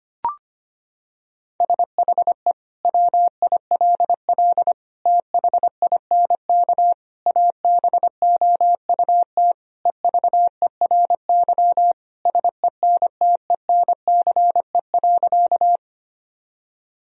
Morse code, heard as a keyed tone, about 25 wpm, 700 Hz. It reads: SHE WILL THINK ABOUT EVERY SENTENCE.
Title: Sentences from Top 1000 Words - Review Part 1 - 25wpm